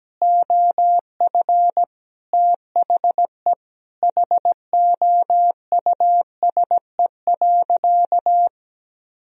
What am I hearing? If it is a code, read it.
Morse code, heard as a keyed tone, about 17 wpm, 700 Hz. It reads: OF THE HOUSE.
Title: Sentences from Top 400 Words and Abbreviations 17wpm